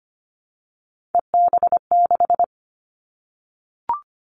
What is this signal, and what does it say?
Morse code, heard as a keyed tone, about 25 wpm, 700 Hz. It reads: EB6
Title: Callsign Prefixes for Top 100 DXCC - 25wpm